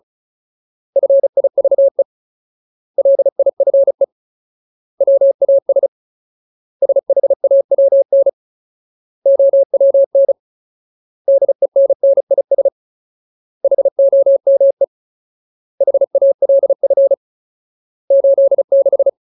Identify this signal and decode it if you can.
Morse code, heard as a keyed tone, about 34 wpm, 550 Hz. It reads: FIVE LIFE WAS SHAWN OWN DENNIS HOME HALF 86
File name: Ultimate Word List / 35WPM Ultimate